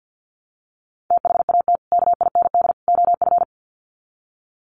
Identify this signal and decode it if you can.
Morse code, heard as a keyed tone, about 48 wpm, 700 Hz. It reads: THAT KIND OF
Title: Sets of 3 Words 50wpm